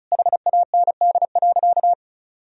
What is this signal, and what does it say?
Morse code, heard as a keyed tone, about 36 wpm, 700 Hz. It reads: HAND.
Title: Sentences from Top 1000 Words - Review Part 1 - 35wpm